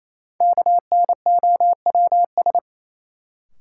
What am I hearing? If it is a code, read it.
Morse code, heard as a keyed tone, about 28 wpm, 700 Hz. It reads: KNOWS
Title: Sentences from Top 1000 Words - Review Part 1 - New Word Drill - 28wpm